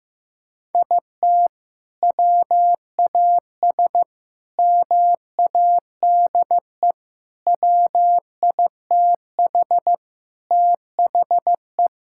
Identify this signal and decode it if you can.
Morse code, heard as a keyed tone, about 16 wpm, 700 Hz. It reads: IT WAS MADE WITH THE
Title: Sentences from Top 1000 Words - Review Part 5 - 15wpm